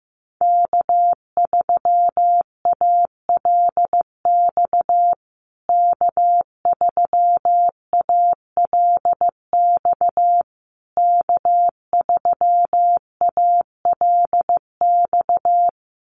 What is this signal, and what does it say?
Morse code, heard as a keyed tone, about 15 wpm, 700 Hz. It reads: K3ALX K3ALX K3ALX
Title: USA Call Signs Repeated 3x 15wpm